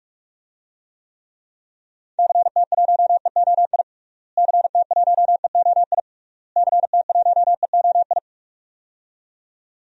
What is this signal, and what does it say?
Morse code, heard as a keyed tone, about 46 wpm, 700 Hz. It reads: CT1EOI CT1EOI CT1EOI
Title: DXCC-100 Call Signs - 6 Characters - Repeated 3x - 45wpm